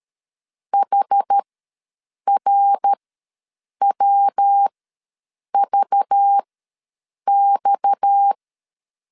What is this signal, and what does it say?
Morse code, heard as a keyed tone, about 13 wpm, spaced as for 7 wpm, 800 Hz. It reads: HRWVX
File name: LEZIONI / 8 YJ45 / 8 YJ45 ws 12wpm 3146